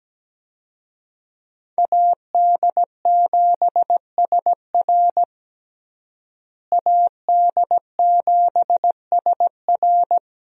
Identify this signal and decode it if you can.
Morse code, heard as a keyed tone, about 17 wpm, 700 Hz. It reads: AD7SR AD7SR
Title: USA Call Signs - 5 Characters - Repeated 3x - 17wpm